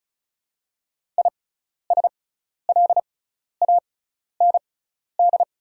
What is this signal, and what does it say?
Morse code, heard as a keyed tone, about 36 wpm, spaced as for 12 wpm, 690 Hz. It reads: ISLAND
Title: 6x Character Spacing - Top 500 Words - 35wpm